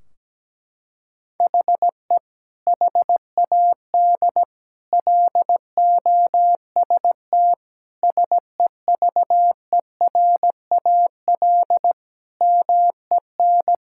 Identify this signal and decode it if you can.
Morse code, heard as a keyed tone, about 17 wpm, 700 Hz. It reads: HE HAD LOST SEVERAL MEN
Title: Sentences from Top 1600 Words 17wpm